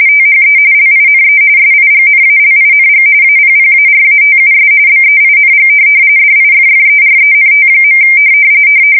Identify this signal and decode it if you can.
RTTY, about 45.45 baud. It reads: THE LINE SIGNALING APPARATUS BEING DIFFERENT.
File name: fsk170 / printing-telegraph